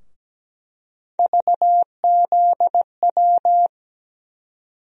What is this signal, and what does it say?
Morse code, heard as a keyed tone, about 17 wpm, 700 Hz. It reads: VZW
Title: USA Call Sign Postfixes - 3 Letters 17wpm